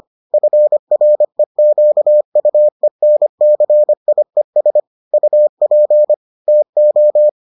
Morse code, heard as a keyed tone, about 25 wpm, 600 Hz. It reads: FREQUENCIES UP TO